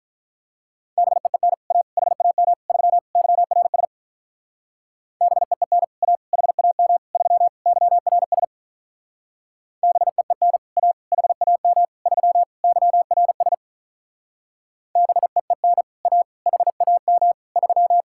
Morse code, heard as a keyed tone, about 46 wpm, 700 Hz. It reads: BEEN A HAM 3 YRS BEEN A HAM 3 YRS BEEN A HAM 3 YRS BEEN A HAM 3